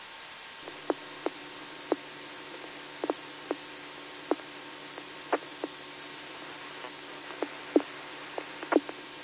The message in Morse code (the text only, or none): none